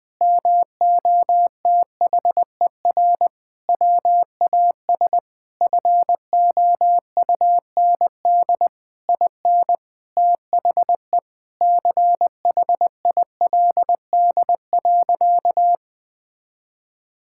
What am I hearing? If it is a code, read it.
Morse code, heard as a keyed tone, about 21 wpm, 700 Hz. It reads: MOTHER WAS FOUND IN THE CHILD.
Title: Sentences from Top 1000 Words - Review Part 5 - 20wpm